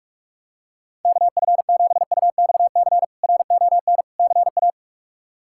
Morse code, heard as a keyed tone, about 42 wpm, 700 Hz. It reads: KF7UXC RON CA
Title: K1USN SST QSO Practice - 45wpm